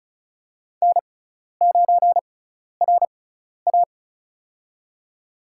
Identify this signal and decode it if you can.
Morse code, heard as a keyed tone, about 36 wpm, spaced as for 12 wpm, 700 Hz. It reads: N9RA
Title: USA Call Signs - 4 Characters - 6x Character Spacing - 35wpm